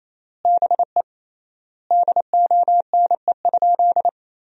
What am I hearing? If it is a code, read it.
Morse code, heard as a keyed tone, about 28 wpm, 700 Hz. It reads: BE DONE?
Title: 3x Word-Spacing - Sentences from Top 500 Words - 28wpm